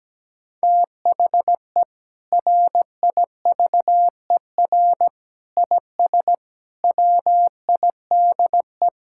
Morse code, heard as a keyed tone, about 17 wpm, 700 Hz. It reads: THE RIVER IS WIDE
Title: Sentences from Top 1000 Words - Review Part 6 - 17wpm